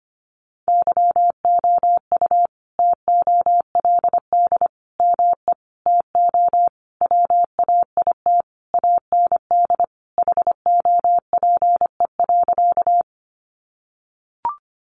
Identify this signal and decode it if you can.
Morse code, heard as a keyed tone, about 25 wpm, 700 Hz. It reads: YOU TOLD ME TO WAIT AND HOPE.
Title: Sentences from Top 1000 Words 25wpm